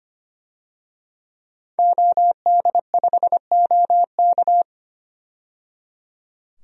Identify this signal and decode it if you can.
Morse code, heard as a keyed tone, about 25 wpm, 700 Hz. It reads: OD5OK
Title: DXCC-100 Call Signs - 5 Characters - 25wpm